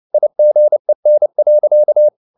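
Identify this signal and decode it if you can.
Morse code, heard as a keyed tone, about 29 wpm, 600 Hz. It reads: IGEN.
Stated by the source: Fast (QRQ) Morse at 29 wpm, from CW med Gnister.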